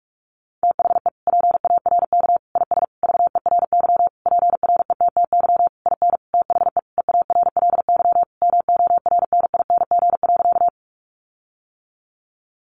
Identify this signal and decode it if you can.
Morse code, heard as a keyed tone, about 44 wpm, 700 Hz. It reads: THE PARK IS VERY PRETTY IN THE EARLY MORNING.
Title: Sentences from Top 1300 Words 45wpm